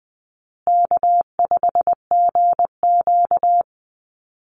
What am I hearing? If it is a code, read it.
Morse code, heard as a keyed tone, about 20 wpm, 700 Hz. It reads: K5GQ
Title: Call Signs Repeated 3x - 20wpm